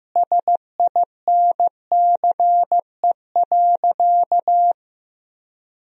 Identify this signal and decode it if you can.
Morse code, heard as a keyed tone, about 15 wpm, 700 Hz. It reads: SINCE.
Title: 3x Word-Spacing - Sentences from Top 400 Words - 15wpm